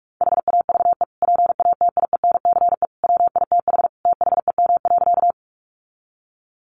Morse code, heard as a keyed tone, about 44 wpm, 700 Hz. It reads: HAVE PATIENCE WITH THEM.
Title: Sentences from Top 1300 Words 45wpm